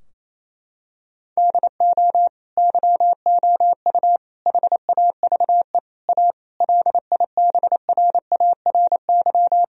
Morse code, heard as a keyed tone, about 27 wpm, 700 Hz. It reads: DO YOU HAVE A LIBRARY
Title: Sentences from Top 1000 Words - Review Part 4 - 28wpm